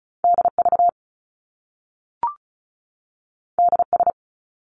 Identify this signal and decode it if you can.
Morse code, heard as a keyed tone, about 36 wpm, 700 Hz. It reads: DV DS
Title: Top Two Letter Combinations in Words 35wpm